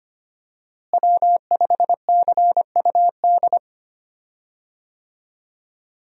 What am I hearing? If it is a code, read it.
Morse code, heard as a keyed tone, about 25 wpm, 700 Hz. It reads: W5CUD
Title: USA Call Signs - 5 Characters - Repeated 2x - 25wpm